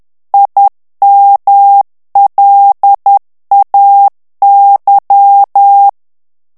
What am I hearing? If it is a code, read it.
Morse code, heard as a keyed tone, about 10 wpm, 800 Hz. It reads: IMLAY
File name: ANGLAIS texte clair 3 vitesse 10 mots minute